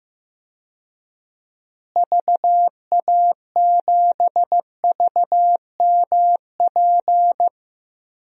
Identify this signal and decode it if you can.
Morse code, heard as a keyed tone, about 15 wpm, 700 Hz. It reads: VA7VMP
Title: DXCC-100 Call Signs - 6 Characters - Repeated 2x - 15wpm